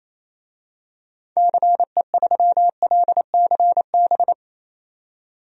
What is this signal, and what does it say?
Morse code, heard as a keyed tone, about 28 wpm, 700 Hz. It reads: CE3LCB
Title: DXCC-100 Call Signs - 6 Characters - 28wpm